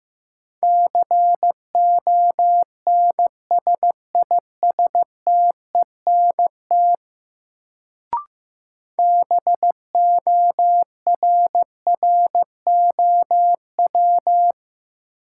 Morse code, heard as a keyed tone, about 15 wpm, 700 Hz. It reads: CONSISTENT BORROW
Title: Sentences from Top 1500 Words - New Word Drill - 15wpm